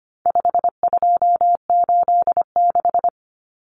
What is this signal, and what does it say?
Morse code, heard as a keyed tone, about 25 wpm, 700 Hz. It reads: 5286
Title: Numbers - 4-Digit 25wpm